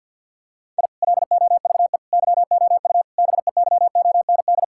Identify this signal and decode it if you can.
Morse code, heard as a keyed tone, about 50 wpm, 700 Hz. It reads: I LOVE YOU BEYOND
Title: Sentences from Top 1000 Words - Review Part 6 - 50wpm